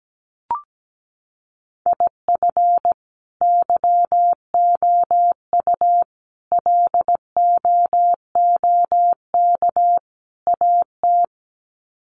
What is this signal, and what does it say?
Morse code, heard as a keyed tone, about 17 wpm, 700 Hz. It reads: IF YOU LOOK AT
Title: Sets of 4 Words 17wpm